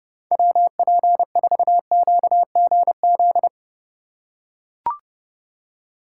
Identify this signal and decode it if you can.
Morse code, heard as a keyed tone, about 28 wpm, 700 Hz. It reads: WP4QGZ